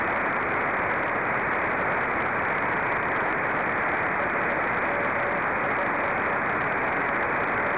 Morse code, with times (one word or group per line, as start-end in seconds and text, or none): none